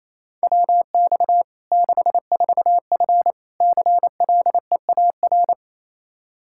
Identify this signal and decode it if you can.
Morse code, heard as a keyed tone, about 28 wpm, 700 Hz. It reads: WX 64F CLEAR
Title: Practice QSO - Exchange 2 - Info - 28wpm